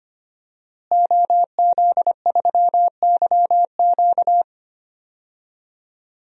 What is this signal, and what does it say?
Morse code, heard as a keyed tone, about 25 wpm, 700 Hz. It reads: OZ3YQ
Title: DXCC-100 Call Signs - 5 Characters - Repeated 2x - 25wpm